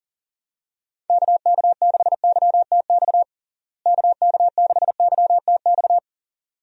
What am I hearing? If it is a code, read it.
Morse code, heard as a keyed tone, about 37 wpm, 700 Hz. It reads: KK6YTX KK6YTX
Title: USA Call Signs - 6 Characters - Repeated 2x - 40wpm